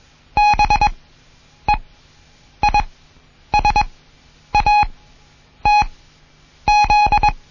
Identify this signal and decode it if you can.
Morse code, heard as a keyed tone, about 21 wpm, spaced as for 9 wpm, 830 Hz. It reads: BEISATZ
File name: Morsen / Kurs A 20